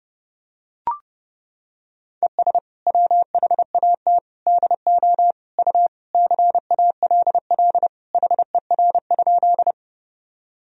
Morse code, heard as a keyed tone, about 30 wpm, 700 Hz. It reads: ES WHAT DO U CALL HER?
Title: Sentences from Top 100 Words and Abbreviations 30wpm